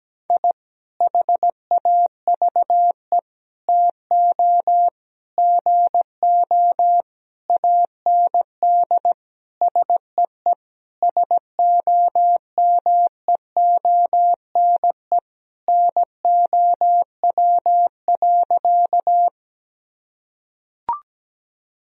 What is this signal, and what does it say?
Morse code, heard as a keyed tone, about 17 wpm, 700 Hz. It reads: I HAVE TO GO AND SEE SOMEONE NOW.
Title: Sentences from Top 1000 Words - Review Part 4 - 17wpm